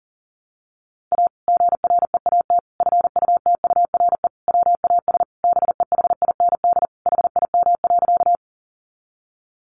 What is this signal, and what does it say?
Morse code, heard as a keyed tone, about 40 wpm, 700 Hz. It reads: A GREAT FUTURE WAS BEHIND HIM.
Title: Sentences from Top 700 Words 40wpm